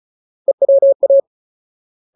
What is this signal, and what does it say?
Morse code, heard as a keyed tone, about 34 wpm, 550 Hz. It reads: EWA
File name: Ultimate Word List / 35WPM Ultimate